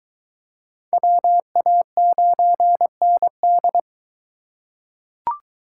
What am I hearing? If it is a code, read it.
Morse code, heard as a keyed tone, about 23 wpm, 700 Hz. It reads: WA9ND